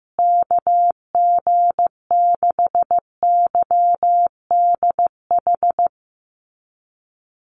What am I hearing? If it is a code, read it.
Morse code, heard as a keyed tone, about 15 wpm, 700 Hz. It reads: KG6YDH
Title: USA Call Signs Repeated 3x 15wpm